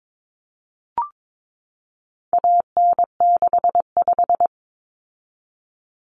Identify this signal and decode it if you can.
Morse code, heard as a keyed tone, about 22 wpm, 700 Hz. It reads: AN65
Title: Common Cut Numbers - 4-Digit 22wpm